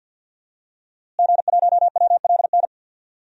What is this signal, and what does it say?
Morse code, heard as a keyed tone, about 47 wpm, 700 Hz. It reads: G1WLN
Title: DXCC-100 Call Signs - 5 Characters - Repeated 3x - 50wpm